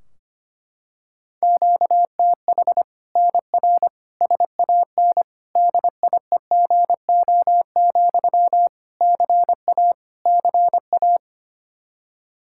Morse code, heard as a keyed tone, about 25 wpm, 700 Hz. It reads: QTH NR SAN DIEGO, CA CA
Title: Practice QSO - Exchange 1 - Beginning - 25wpm